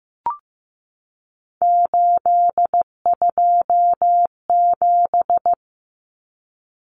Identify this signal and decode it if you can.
Morse code, heard as a keyed tone, about 15 wpm, 700 Hz. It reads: 827
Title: Common Cut Numbers - 3-Digit 15wpm